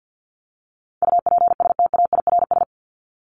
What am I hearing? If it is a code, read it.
Morse code, heard as a keyed tone, about 48 wpm, 700 Hz. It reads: UPSTAIRS
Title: Top 1751-2000 Words 50wpm